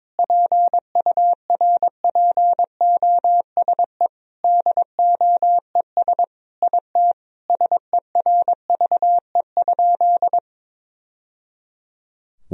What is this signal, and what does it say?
Morse code, heard as a keyed tone, about 22 wpm, 700 Hz. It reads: PURPOSE DOES IT SERVE?
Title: Sentences from Top 1000 Words - Review Part 2 - 22wpm